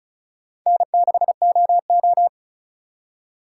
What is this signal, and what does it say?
Morse code, heard as a keyed tone, about 47 wpm, 700 Hz. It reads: N6OO